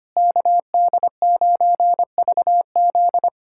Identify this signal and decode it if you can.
Morse code, heard as a keyed tone, about 25 wpm, 700 Hz. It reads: KD9VZ
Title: USA Callsign DE Callsign - 5 Characters - 25wpm